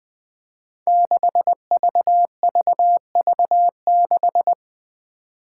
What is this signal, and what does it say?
Morse code, heard as a keyed tone, about 20 wpm, 700 Hz. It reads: 6VVV6
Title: Five Challenging Letter-Number Combinations 20wpm